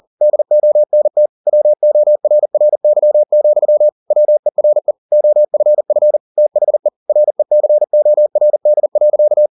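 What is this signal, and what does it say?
Morse code, heard as a keyed tone, about 40 wpm, 600 Hz. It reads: DONT WORRY, WERE OFF THE RECORD.
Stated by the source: Morse code at 40 wpm, practice text, CW med Gnister.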